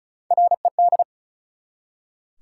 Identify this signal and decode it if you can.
Morse code, heard as a keyed tone, about 36 wpm, 700 Hz. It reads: RED
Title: Three Letter Words - 35wpm